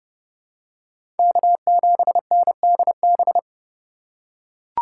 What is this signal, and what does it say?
Morse code, heard as a keyed tone, about 30 wpm, 700 Hz. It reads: K7NDB
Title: USA Call Signs - 5 Characters - 2x Character Spacing - 30wpm